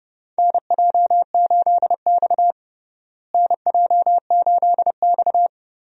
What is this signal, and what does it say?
Morse code, heard as a keyed tone, about 30 wpm, 700 Hz. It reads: NJ8X NJ8X
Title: USA Call Signs - 4 Characters - Repeated 2x - 30wpm